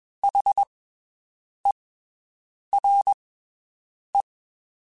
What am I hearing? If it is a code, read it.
Morse code, heard as a keyed tone, about 21 wpm, spaced as for 7 wpm, 800 Hz. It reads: HERE
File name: (spaced) / QSOs Spaced-mp3s / WD0BGP 1 @20 sp